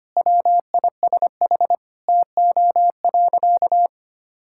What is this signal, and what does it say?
Morse code, heard as a keyed tone, about 25 wpm, 700 Hz. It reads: WISH TO.
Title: Sentences from Top 1600 Words 25wpm